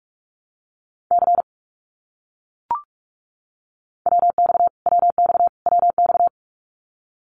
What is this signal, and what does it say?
Morse code, heard as a keyed tone, about 44 wpm, 700 Hz. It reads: C WX WX WX
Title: Abbreviations Repeated 3x - 45wpm